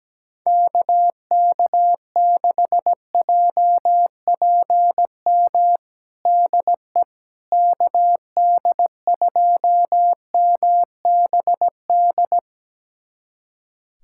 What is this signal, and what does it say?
Morse code, heard as a keyed tone, about 17 wpm, 700 Hz. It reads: KK6JPM DE KD2MBD
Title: USA Callsign DE Callsign - 6 Characters - 17wpm